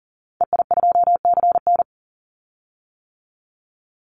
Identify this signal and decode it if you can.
Morse code, heard as a keyed tone, about 40 wpm, 700 Hz. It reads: EI2CN
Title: Call Signs - 40wpm